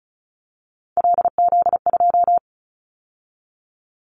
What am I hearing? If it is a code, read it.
Morse code, heard as a keyed tone, about 36 wpm, 700 Hz. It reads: LZ2
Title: Callsign Prefixes for Top 50 DXCC - 35wpm